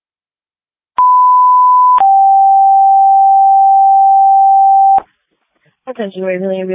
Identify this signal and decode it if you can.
Two-tone page, heard as a one-second tone, then a three-second tone, about 1000 then 780 Hz, with speech. It is heard from 1.0 to 5.0 s.